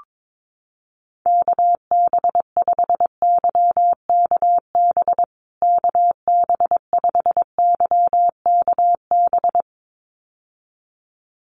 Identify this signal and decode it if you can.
Morse code, heard as a keyed tone, about 22 wpm, 700 Hz. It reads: KB5YKB KB5YKB
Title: USA Call Signs Repeated 2x 22wpm